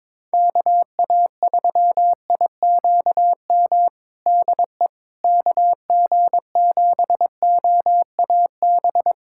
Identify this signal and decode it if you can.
Morse code, heard as a keyed tone, about 22 wpm, 700 Hz. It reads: KA3IQM DE KG7OAB
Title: USA Callsign DE Callsign - 6 Characters - 22wpm